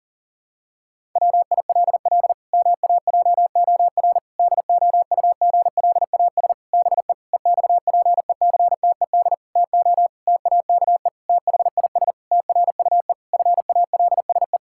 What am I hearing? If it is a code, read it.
Morse code, heard as a keyed tone, about 37 wpm, 700 Hz. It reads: WILL MAJOR DOUGLAS BE EXPECTED TO TAKE THIS TRUE FALSE
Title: Pangram Challenge v2 - 40wpm